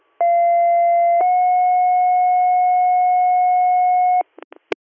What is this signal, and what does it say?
Two-tone page, heard as a one-second tone, then a three-second tone, about 690 then 730 Hz. It is heard from 0.2 to 4.2 s.